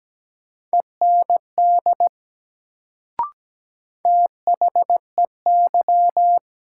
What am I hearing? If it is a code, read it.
Morse code, heard as a keyed tone, about 17 wpm, 700 Hz. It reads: END THEY
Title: Sets of 2 Words - New Word Drill - 17wpm